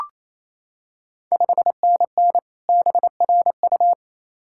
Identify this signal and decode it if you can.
Morse code, heard as a keyed tone, about 28 wpm, 690 Hz. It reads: 5NN BRU
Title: Ontario QSO Party Exchange 28wpm